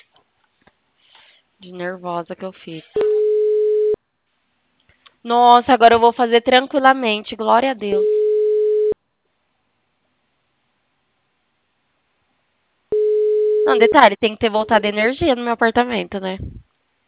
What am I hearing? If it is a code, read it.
Ringback tone (Europe, 425 Hz), with speech.